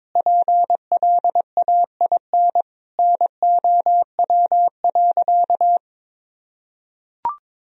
Morse code, heard as a keyed tone, about 22 wpm, 700 Hz. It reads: PLAIN NOW.
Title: Sentences from Top 1000 Words - Review Part 3 - 22wpm